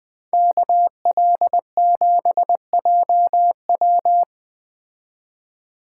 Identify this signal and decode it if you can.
Morse code, heard as a keyed tone, about 20 wpm, 700 Hz. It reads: KL7JW